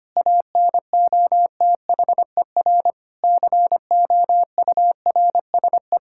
Morse code, heard as a keyed tone, about 25 wpm, 700 Hz. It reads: ANOTHER COURSE
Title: Sentences from Top 1000 Words - Review Part 6 - 25wpm